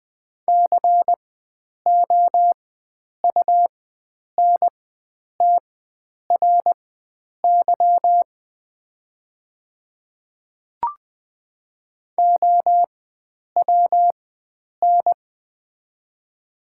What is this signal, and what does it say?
Morse code, heard as a keyed tone, about 20 wpm, spaced as for 9 wpm, 700 Hz. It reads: COUNTRY OWN
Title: 4x Character Spacing - Top 300 Words - 20wpm